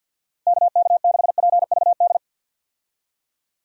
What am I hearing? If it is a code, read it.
Morse code, heard as a keyed tone, about 52 wpm, 700 Hz. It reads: KK6PVD
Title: USA Call Signs - 6 Characters - 50wpm